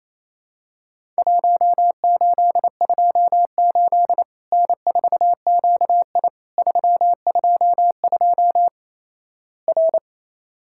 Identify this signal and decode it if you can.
Morse code, heard as a keyed tone, about 28 wpm, 700 Hz. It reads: 1828 N4QI 322 R
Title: Worked All Europe DX Contest - QTC Exchange - 28wpm